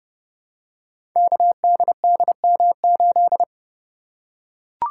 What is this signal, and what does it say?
Morse code, heard as a keyed tone, about 30 wpm, 700 Hz. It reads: KDDM8
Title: Code Groups - 5 Letter:Numbers - 30wpm